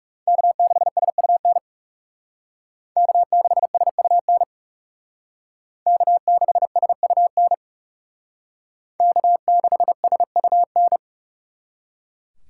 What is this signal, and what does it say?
Morse code, heard as a keyed tone, about 39 wpm, 700 Hz. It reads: K6SUN K6SUN K6SUN K6SUN